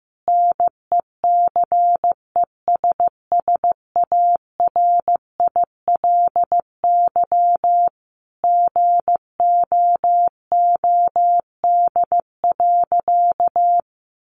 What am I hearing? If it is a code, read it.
Morse code, heard as a keyed tone, about 15 wpm, 700 Hz. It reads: NECESSARILY GOOD.
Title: Sentences from Top 1400 Words 15wpm